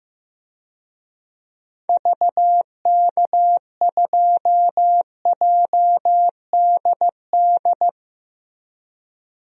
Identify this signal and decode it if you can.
Morse code, heard as a keyed tone, about 15 wpm, 700 Hz. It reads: VK2JDD
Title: DXCC-100 Call Signs - 6 Characters - Repeated 2x - 15wpm